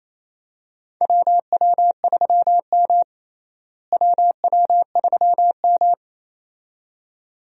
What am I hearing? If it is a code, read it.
Morse code, heard as a keyed tone, about 28 wpm, 700 Hz. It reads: WW3M WW3M
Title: USA Call Signs - 4 Characters - Repeated 2x - 28wpm